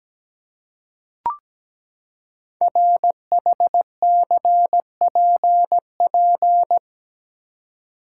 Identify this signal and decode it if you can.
Morse code, heard as a keyed tone, about 17 wpm, 700 Hz. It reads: RHCPP
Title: Code Groups - 5 Letters - 17wpm